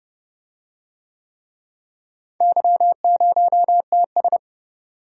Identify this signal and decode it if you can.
Morse code, heard as a keyed tone, about 31 wpm, 700 Hz. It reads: Y0TS